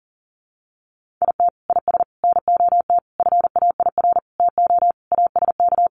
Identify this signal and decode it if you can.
Morse code, heard as a keyed tone, about 40 wpm, 700 Hz. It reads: IT IS NOT FAIR TO ASK
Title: Sentences from Top 1300 Words 40wpm